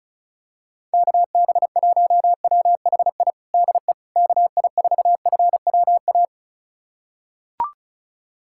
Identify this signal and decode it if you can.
Morse code, heard as a keyed tone, about 34 wpm, 700 Hz. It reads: KB1WHI DE KI4FWA
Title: USA Callsign DE Callsign - 6 Characters - 35wpm